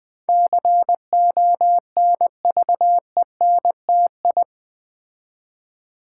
Morse code, heard as a keyed tone, about 20 wpm, 700 Hz. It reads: CONVENTI
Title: Build A Word - 4 Letters At A Time - 10-Letter Words - 20wpm